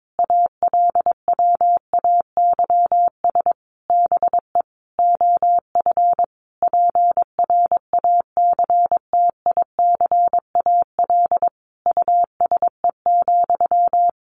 Morse code, heard as a keyed tone, about 22 wpm, 700 Hz. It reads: ALWAYS BE OF PRACTICAL USE,
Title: Sentences from Top 1400 Words 22wpm